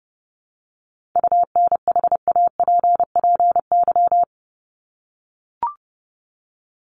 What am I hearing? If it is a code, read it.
Morse code, heard as a keyed tone, about 30 wpm, 700 Hz. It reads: UNHAPPY
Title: Top 1501-1750 Words 30wpm